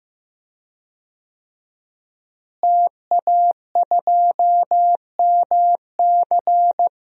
Morse code, heard as a keyed tone, about 15 wpm, 700 Hz. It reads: TA2MC
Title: DXCC-100 Call Signs - 5 Characters - Repeated 3x - 15wpm